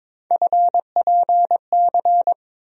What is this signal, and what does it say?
Morse code, heard as a keyed tone, about 22 wpm, 700 Hz. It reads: FPC
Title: USA Call Sign Postfixes - 3 Letters 22wpm